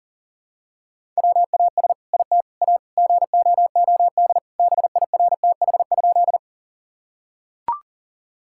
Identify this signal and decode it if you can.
Morse code, heard as a keyed tone, about 40 wpm, 700 Hz. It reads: WAS IT A GOOD BIRTH?
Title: Sentences from Top 1000 Words - Review Part 5 - 40wpm